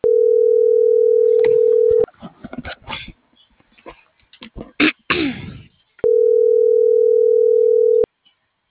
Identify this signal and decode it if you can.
Ringback tone (North America), 440 + 480 Hz.